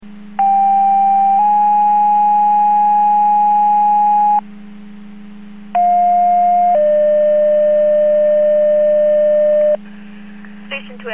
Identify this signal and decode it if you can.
Two-tone page, heard as a one-second tone, then a three-second tone, about 790 then 830 Hz, about 710 then 600 Hz, with speech. It is heard from 0.4 to 9.8 s.